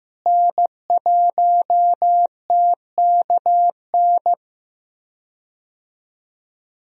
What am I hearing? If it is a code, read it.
Morse code, heard as a keyed tone, about 15 wpm, 700 Hz. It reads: N1TKN
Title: USA Call Signs - 5 Characters - Repeated 2x - 15wpm